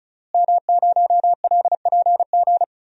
Morse code, heard as a keyed tone, about 34 wpm, 700 Hz. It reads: M0LPG
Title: DXCC-100 Call Signs - 5 Characters - Repeated 2x - 35wpm